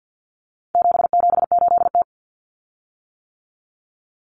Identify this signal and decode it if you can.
Morse code, heard as a keyed tone, about 50 wpm, 700 Hz. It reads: 778T
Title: Common Cut Numbers - 4-Digit 50wpm